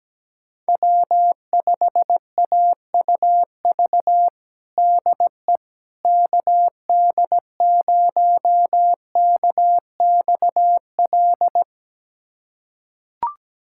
Morse code, heard as a keyed tone, about 17 wpm, 700 Hz. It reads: W5AUV DE KD0KXL
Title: USA Callsign DE Callsign 17wpm